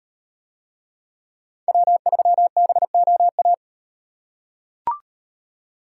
Morse code, heard as a keyed tone, about 37 wpm, 700 Hz. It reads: W3BOA